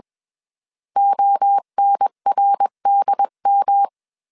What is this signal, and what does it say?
Morse code, heard as a keyed tone, about 21 wpm, 800 Hz. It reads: ONRDM